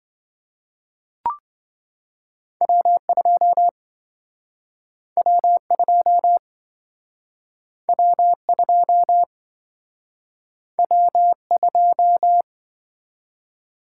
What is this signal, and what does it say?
Morse code, heard as a keyed tone, about 26 wpm, 700 Hz. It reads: W2 W2 W2 W2